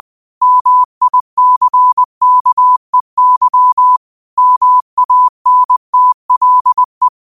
Morse code, heard as a keyed tone, about 20 wpm, 1000 Hz. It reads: MICKEY MANTLE
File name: qotdfiles / 20wpm